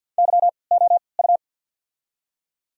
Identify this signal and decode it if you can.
Morse code, heard as a keyed tone, about 54 wpm, 700 Hz. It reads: Y O U
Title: 3x Character Spacing - Top 300 Words - 50wpm